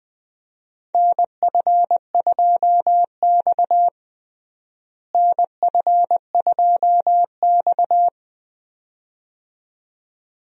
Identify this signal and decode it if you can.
Morse code, heard as a keyed tone, about 21 wpm, 700 Hz. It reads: NF2X NF2X
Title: USA Call Signs - 4 Characters - Repeated 2x - 20wpm